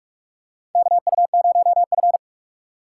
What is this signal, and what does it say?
Morse code, heard as a keyed tone, about 46 wpm, 700 Hz. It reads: KU0F
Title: USA Call Signs - 3x Character Spacing - 45wpm